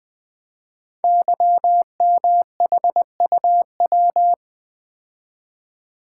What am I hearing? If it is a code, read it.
Morse code, heard as a keyed tone, about 20 wpm, 700 Hz. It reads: YMHUW
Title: Code Groups - 5 Letter:Numbers - 20wpm